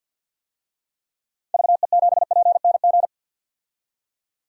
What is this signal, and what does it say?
Morse code, heard as a keyed tone, about 51 wpm, 700 Hz. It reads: VE7PNG
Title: DXCC-100 Call Signs - 6 Characters - 50wpm